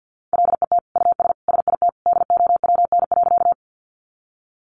Morse code, heard as a keyed tone, about 50 wpm, 700 Hz. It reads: LET US SIT DOWN.
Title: Sentences from Top 500 Words 50wpm